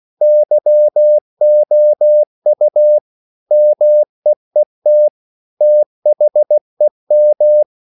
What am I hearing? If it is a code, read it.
Morse code, heard as a keyed tone, about 16 wpm, 600 Hz. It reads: YOU MEET THEM
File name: CW / Practice / EnglishQuotations0216